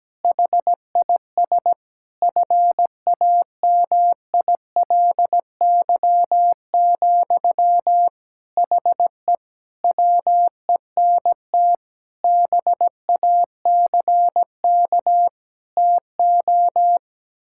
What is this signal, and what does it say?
Morse code, heard as a keyed tone, about 17 wpm, 700 Hz. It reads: HIS FAMILY, HE WENT BACK TO
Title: Sentences from Top 1000 Words - Review Part 6 - 17wpm